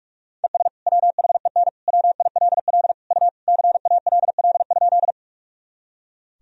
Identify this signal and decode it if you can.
Morse code, heard as a keyed tone, about 46 wpm, 700 Hz. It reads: ES WHEN WILL U CALL?
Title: Sentences from Top 100 Words and Abbreviations 45wpm